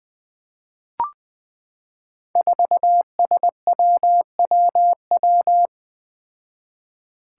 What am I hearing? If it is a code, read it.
Morse code, heard as a keyed tone, about 20 wpm, 700 Hz. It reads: 4SWWW
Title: Five Challenging Letter-Number Combinations 20wpm